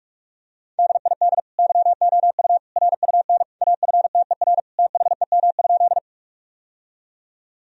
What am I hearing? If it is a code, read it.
Morse code, heard as a keyed tone, about 46 wpm, 700 Hz. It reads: DID YOU RUN AFTER THEM?
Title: Sentences from Top 1000 Words - Review Part 1 - 45wpm